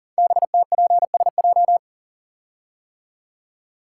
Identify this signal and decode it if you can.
Morse code, heard as a keyed tone, about 42 wpm, 700 Hz. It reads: BTPSJ
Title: Code Groups - 5 Letters - 40wpm